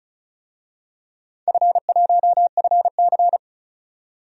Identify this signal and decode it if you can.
Morse code, heard as a keyed tone, about 36 wpm, 700 Hz. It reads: F1FC
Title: DXCC-100 Call Signs - 4 Characters - 35wpm